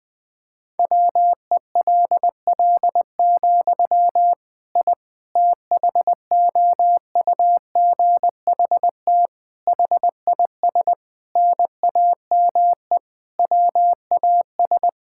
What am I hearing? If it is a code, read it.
Morse code, heard as a keyed tone, about 20 wpm, 700 Hz. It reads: WELL, I THOUGHT HIS NAME WAS
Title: Sentences from Top 1000 Words - Review Part 5 - 20wpm